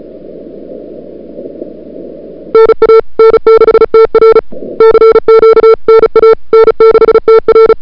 Morse code, heard as a keyed tone, about 34 wpm, 440 Hz. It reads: NA N6TR CQ NA N6TR
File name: n6tr / Fab2007CWSprintHour1